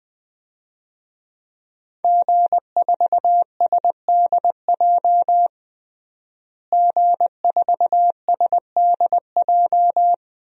Morse code, heard as a keyed tone, about 20 wpm, 700 Hz. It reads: G4SDJ G4SDJ
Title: DXCC-100 Call Signs - 5 Characters - Repeated 2x - 20wpm